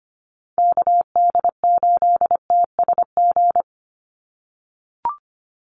Morse code, heard as a keyed tone, about 25 wpm, 700 Hz. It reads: KD8TSG
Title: USA Call Signs 25wpm